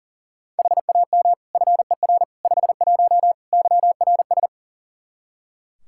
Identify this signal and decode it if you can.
Morse code, heard as a keyed tone, about 40 wpm, 700 Hz. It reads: HAM FER 51 YRS
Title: Practice QSO - Exchange 3 - More Info - 40wpm